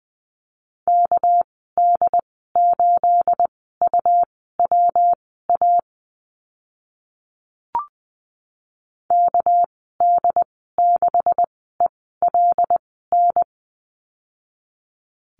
Morse code, heard as a keyed tone, about 20 wpm, spaced as for 15 wpm, 700 Hz. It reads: KD8UWA KD6ELN
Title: 213 Lesson 34g - Number 8 - Call Signs 20wpm